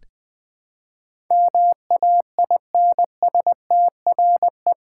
Morse code, heard as a keyed tone, about 21 wpm, 700 Hz. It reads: MAINSTRE
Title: Build A Word - 4 Letters At A Time - 10-Letter Words - 20wpm